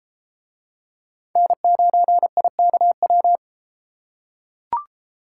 Morse code, heard as a keyed tone, about 30 wpm, 700 Hz. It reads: N9IKW